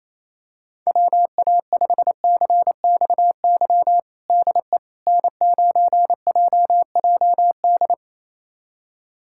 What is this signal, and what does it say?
Morse code, heard as a keyed tone, about 28 wpm, 700 Hz. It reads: WA5CXY DE N9JJD
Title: USA Callsign DE Callsign 28wpm